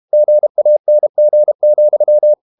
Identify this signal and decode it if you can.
Morse code, heard as a keyed tone, about 32 wpm, 600 Hz. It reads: GANG,
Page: Ferie 32 wpm